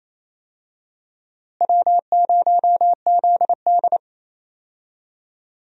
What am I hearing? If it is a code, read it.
Morse code, heard as a keyed tone, about 28 wpm, 700 Hz. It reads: W0ZD
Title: USA Call Signs - 4 Characters - Repeated 2x - 28wpm